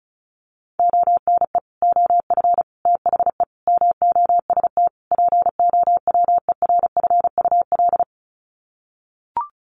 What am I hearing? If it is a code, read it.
Morse code, heard as a keyed tone, about 34 wpm, 700 Hz. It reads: ONE OF THE MOST POWERFUL
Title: Sets of 5 Words Encore 35wpm